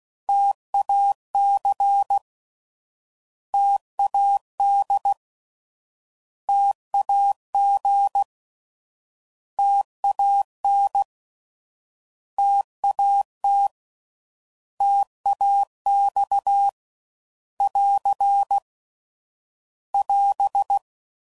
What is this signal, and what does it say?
Morse code, heard as a keyed tone, about 16 wpm, 800 Hz. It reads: TAC TAD TAG TAN TAT TAX <AR> <AS>
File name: Prefixes / By First Letter / Ts